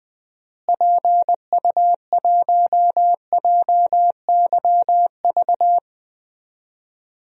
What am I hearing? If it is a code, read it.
Morse code, heard as a keyed tone, about 20 wpm, 690 Hz. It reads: PU1JYV
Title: DXCC-100 Call Signs - 6 Characters - Repeated 2x - 20wpm